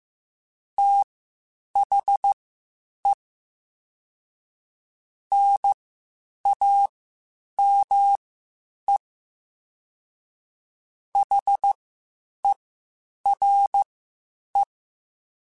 Morse code, heard as a keyed tone, about 15 wpm, spaced as for 8 wpm, 800 Hz. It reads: THE NAME HERE
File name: (spaced) / QSOs Spaced-mp3s / WD0BGP 1 @13 sp